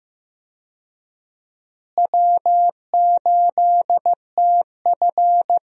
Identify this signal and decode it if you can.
Morse code, heard as a keyed tone, about 15 wpm, 700 Hz. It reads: W8TF
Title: USA Call Signs - 4 Characters - Repeated 3x - 15wpm